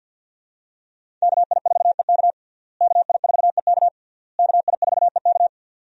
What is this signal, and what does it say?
Morse code, heard as a keyed tone, about 51 wpm, 700 Hz. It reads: KI4EK KI4EK KI4EK
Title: USA Call Signs - 5 Characters - Repeated 3x - 50wpm